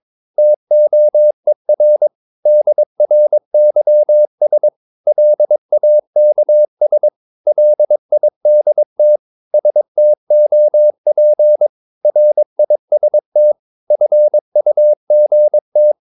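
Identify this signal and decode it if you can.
Morse code, heard as a keyed tone, about 22 wpm, 600 Hz. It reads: TOER DRYS LAKS LIDT STOP RIST FUGT
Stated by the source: Morse practice of short Danish words at 22 wpm, from CW med Gnister.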